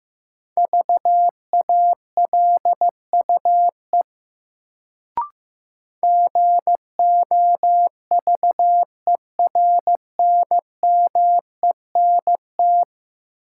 Morse code, heard as a keyed tone, about 15 wpm, 700 Hz. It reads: VALUE GOVERNMENT
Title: Sentences from Top 600 Words - New Word Drill - 15wpm